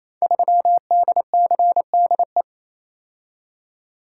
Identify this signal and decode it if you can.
Morse code, heard as a keyed tone, about 28 wpm, 700 Hz. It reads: 3DCDE
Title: Code Groups - 5 Letter:Numbers - 28wpm